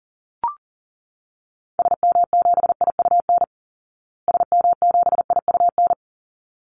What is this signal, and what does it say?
Morse code, heard as a keyed tone, about 40 wpm, 700 Hz. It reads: SM7IUN SM7IUN
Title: Call Signs Repeated 3x - 40wpm